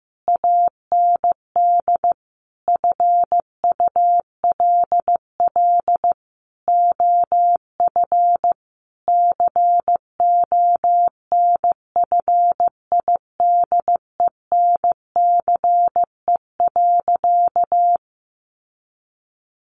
Morse code, heard as a keyed tone, about 15 wpm, 700 Hz. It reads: AND FULL OF CONFIDENCE.
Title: Sentences from Top 1500 Words 15wpm